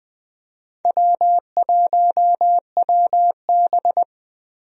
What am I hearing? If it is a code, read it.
Morse code, heard as a keyed tone, about 20 wpm, 700 Hz. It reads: W1WB
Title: USA Call Signs - 4 Characters - Repeated 3x - 20wpm